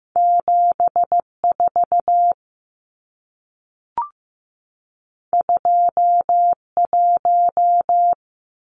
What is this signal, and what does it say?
Morse code, heard as a keyed tone, about 15 wpm, 700 Hz. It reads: 7 4 2 1
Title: Numbers - 2-Digit 15wpm